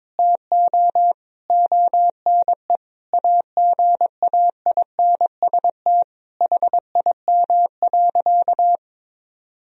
Morse code, heard as a keyed tone, about 22 wpm, 700 Hz. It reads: TO ONE AGAINST HIM.
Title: Sentences from Top 1000 Words - Review Part 3 - 22wpm